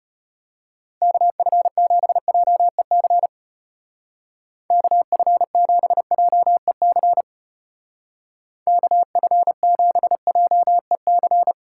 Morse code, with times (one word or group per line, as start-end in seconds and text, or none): 1.0-3.3 KF7JEC
4.7-7.2 KF7JEC
8.7-11.5 KF7JEC